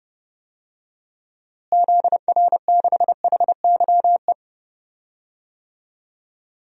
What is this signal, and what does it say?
Morse code, heard as a keyed tone, about 32 wpm, 700 Hz. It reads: ZR6HYE
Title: DXCC-100 Call Signs - 6 Characters - Repeated 2x - 30wpm